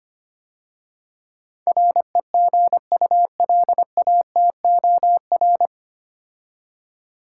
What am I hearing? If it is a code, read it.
Morse code, heard as a keyed tone, about 25 wpm, 700 Hz. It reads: REGULATOR
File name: practice / 25wpm / 9 Letter Words - First 3 Letters Given - 25wpm